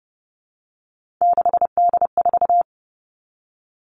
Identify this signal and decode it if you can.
Morse code, heard as a keyed tone, about 30 wpm, 700 Hz. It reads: 6D4
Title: Three Challenging Letter-Number Combinations 30wpm